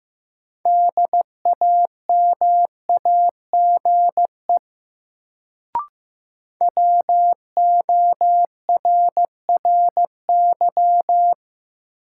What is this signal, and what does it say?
Morse code, heard as a keyed tone, about 15 wpm, 700 Hz. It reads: DAMAGE WORRY
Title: Sentences from Top 1100 Words - New Word Drill Alt - 15wpm